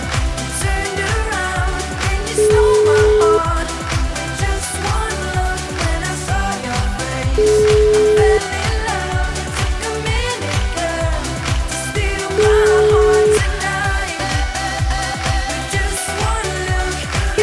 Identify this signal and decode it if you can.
Ringback tone (Europe, 425 Hz), with noise.